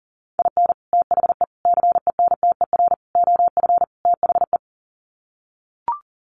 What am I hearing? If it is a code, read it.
Morse code, heard as a keyed tone, about 37 wpm, 700 Hz. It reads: IN THE CENTER OF THE
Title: Sets of 5 Words 40wpm